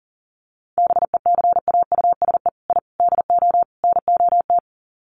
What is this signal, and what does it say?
Morse code, heard as a keyed tone, about 40 wpm, 700 Hz. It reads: BECAUSE I DO NOT
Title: Sets of 4 Words 40wpm